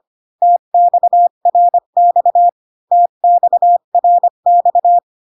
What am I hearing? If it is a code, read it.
Morse code, heard as a keyed tone, about 25 wpm, 700 Hz. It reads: TXRX TXRX